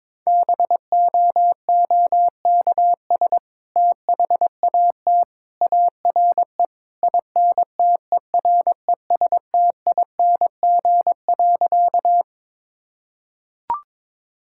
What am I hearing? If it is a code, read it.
Morse code, heard as a keyed tone, about 22 wpm, 700 Hz. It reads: BOOKS THAT ARE INTERESTING.
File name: practice / 22wpm / Sentences with 2-Word Phrases - 22wpm